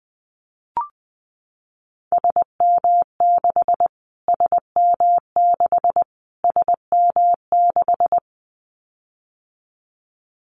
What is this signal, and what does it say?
Morse code, heard as a keyed tone, about 20 wpm, 700 Hz. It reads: SM6 SM6 SM6
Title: Callsign Prefixes for Top 25 DXCC - Repeated 3x - 20wpm